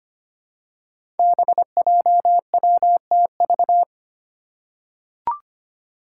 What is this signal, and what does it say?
Morse code, heard as a keyed tone, about 25 wpm, 700 Hz. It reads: BJWTV
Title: Five Challenging Letter-Number Combinations 25wpm